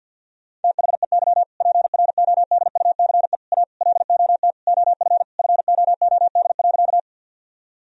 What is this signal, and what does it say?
Morse code, heard as a keyed tone, about 50 wpm, 700 Hz. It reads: THEY PRODUCE A LOT OF FOOD.
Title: Sentences from Top 1000 Words - Review Part 3 - 50wpm